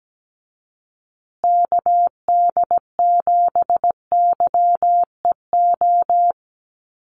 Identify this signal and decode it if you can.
Morse code, heard as a keyed tone, about 17 wpm, 700 Hz. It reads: KD7YEO
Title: USA Call Signs Repeated 2x 17wpm